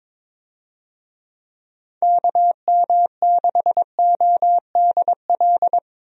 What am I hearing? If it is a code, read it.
Morse code, heard as a keyed tone, about 22 wpm, 700 Hz. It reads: KM6ODL
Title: USA Call Signs - 6 Characters - Repeated 2x - 22wpm